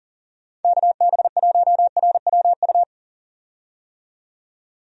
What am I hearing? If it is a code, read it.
Morse code, heard as a keyed tone, about 40 wpm, 700 Hz. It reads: KB1RWU
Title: USA Call Signs - 6 Characters - 40wpm